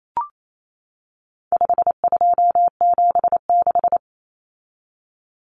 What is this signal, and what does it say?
Morse code, heard as a keyed tone, about 28 wpm, 700 Hz. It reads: 5276
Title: Numbers - 4-Digit 28wpm